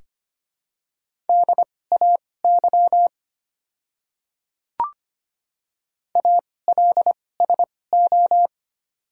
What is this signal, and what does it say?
Morse code, heard as a keyed tone, about 25 wpm, spaced as for 18 wpm, 700 Hz. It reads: DAY ALSO
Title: 2x Character Spacing - Top 200 Words - 25wpm